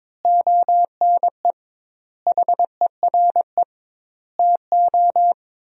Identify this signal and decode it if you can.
Morse code, heard as a keyed tone, about 22 wpm, 700 Hz. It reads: ONE HERE TO
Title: 2x Word-Spacing - Sentences from Top 200 Words - 22wpm